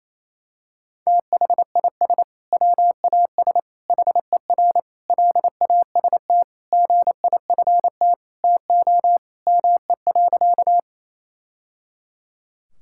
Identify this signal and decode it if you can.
Morse code, heard as a keyed tone, about 27 wpm, 700 Hz. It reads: THIS WAS HER LAST GIFT TO ME.
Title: Sentences from Top 1000 Words - Review Part 6 - 28wpm